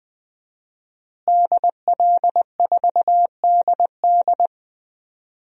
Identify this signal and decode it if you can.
Morse code, heard as a keyed tone, about 20 wpm, 700 Hz. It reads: DL4DD
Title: DXCC-100 Call Signs - 5 Characters - Repeated 2x - 20wpm